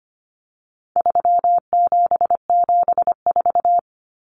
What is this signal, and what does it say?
Morse code, heard as a keyed tone, about 25 wpm, 700 Hz. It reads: 3774
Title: Common Cut Numbers - 4-Digit 25wpm